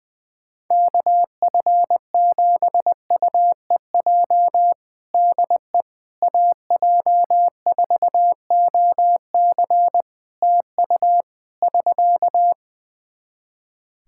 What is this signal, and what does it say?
Morse code, heard as a keyed tone, about 20 wpm, 700 Hz. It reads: KF7UEJ DE AJ4OC TU <SK>
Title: Practice QSO - Exchange 4 - Ending - 20wpm